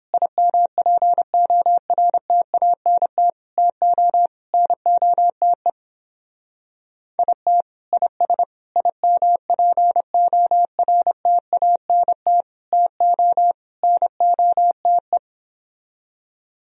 Morse code, heard as a keyed tone, about 29 wpm, 700 Hz. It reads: IMPORTANT TO NOTE IT IS IMPORTANT TO NOTE